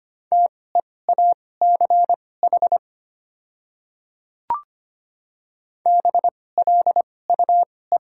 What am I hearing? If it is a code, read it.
Morse code, heard as a keyed tone, about 25 wpm, spaced as for 18 wpm, 700 Hz. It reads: TEACH BLUE
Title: 2x Character Spacing - Top 500 Words - 25wpm